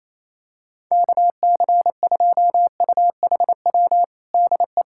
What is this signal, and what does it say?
Morse code, heard as a keyed tone, about 27 wpm, 700 Hz. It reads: KC2UHW DE
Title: Practice QSO - Exchange 4 - Ending - 28wpm